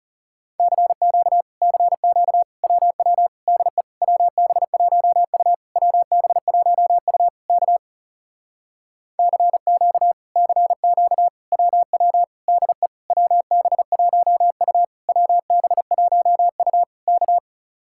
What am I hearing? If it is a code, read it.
Morse code, heard as a keyed tone, about 40 wpm, 700 Hz. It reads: CQ CQ WW DE WB1U WB1U K CQ CQ WW DE WB1U WB1U K